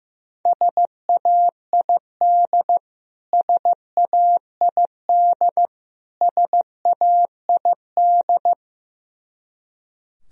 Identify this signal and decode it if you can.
Morse code, heard as a keyed tone, about 15 wpm, 700 Hz. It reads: SAID SAID SAID
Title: Top 100 Words Repeated 15wpm